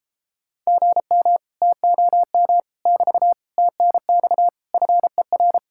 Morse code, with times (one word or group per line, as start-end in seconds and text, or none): 0.7-1.4 GM
1.6-2.6 TOM
2.9-3.3 <BT>
3.6-4.5 TNX
4.7-5.6 FER